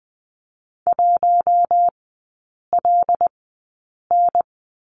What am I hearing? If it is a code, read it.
Morse code, heard as a keyed tone, about 20 wpm, spaced as for 8 wpm, 700 Hz. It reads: 1LN
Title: 074 Lesson 13e - Letter L - Three Characters 20wpm